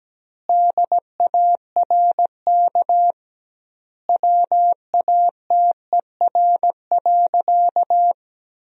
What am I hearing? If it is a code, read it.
Morse code, heard as a keyed tone, about 17 wpm, 700 Hz. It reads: DARK WATER.
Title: 2x Word-Spacing - Sentences from Top 500 Words - 17wpm